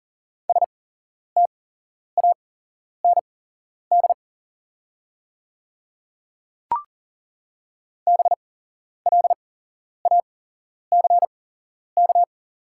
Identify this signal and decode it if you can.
Morse code, heard as a keyed tone, about 40 wpm, spaced as for 11 wpm, 690 Hz. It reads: STAND BLACK
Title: 8x Character Spacing - Top 500 Words - 40wpm